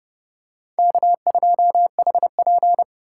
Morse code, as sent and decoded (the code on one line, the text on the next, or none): -.- ..--- .... .--.
K2HP